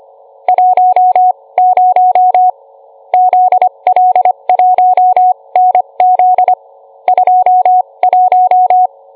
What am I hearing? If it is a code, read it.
Morse code, heard as a keyed tone, about 25 wpm, spaced as for 21 wpm, 730 Hz. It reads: 10 ZL1NZ 21